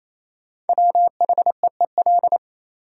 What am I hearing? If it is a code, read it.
Morse code, heard as a keyed tone, about 28 wpm, 700 Hz. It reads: WHEEL
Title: Sentences from Top 500 Words - New Word Drill - 28wpm